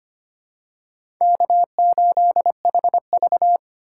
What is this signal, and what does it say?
Morse code, heard as a keyed tone, about 25 wpm, 700 Hz. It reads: K8HV
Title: USA Call Signs - 4 Characters - 3x Character Spacing - 25wpm